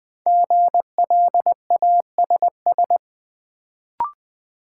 Morse code, heard as a keyed tone, about 20 wpm, 700 Hz. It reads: GLASS
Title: Sentences from Top 1000 Words - Review Part 3 - New Word Drill - 20wpm